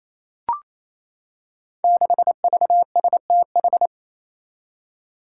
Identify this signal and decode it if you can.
Morse code, heard as a keyed tone, about 27 wpm, 700 Hz. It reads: 6VSTH
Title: Five Challenging Letter-Number Combinations 28wpm